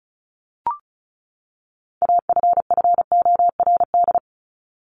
Morse code, heard as a keyed tone, about 34 wpm, 700 Hz. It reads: AFFORD
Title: Top 1001-1250 Words 35wpm